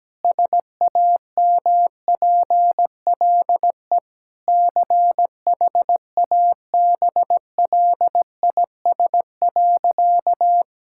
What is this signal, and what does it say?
Morse code, heard as a keyed tone, about 17 wpm, 700 Hz. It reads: SAMPLE CHABLIS.
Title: Pangram Challenge v2 - 17wpm